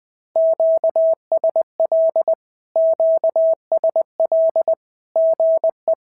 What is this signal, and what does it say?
Morse code, heard as a keyed tone, about 20 wpm, 650 Hz. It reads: QSL QSL GE
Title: SOTA QSO Practice 20wpm